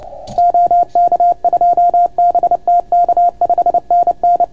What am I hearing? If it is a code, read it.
Morse code, heard as a keyed tone, about 29 wpm, 690 Hz. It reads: OK2BTK 5NN